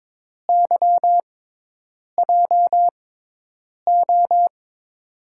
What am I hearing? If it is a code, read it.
Morse code, heard as a keyed tone, about 22 wpm, spaced as for 8 wpm, 700 Hz. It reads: YJO
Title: USA Call Signs - 5 Characters - 6x Character Spacing - 22wpm